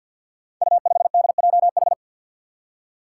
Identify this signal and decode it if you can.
Morse code, heard as a keyed tone, about 52 wpm, 700 Hz. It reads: U5DJH
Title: Five Challenging Letter-Number Combinations 50wpm